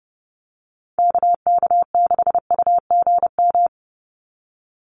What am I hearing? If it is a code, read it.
Morse code, heard as a keyed tone, about 30 wpm, 700 Hz. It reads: KK6UGM
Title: USA Call Signs 30wpm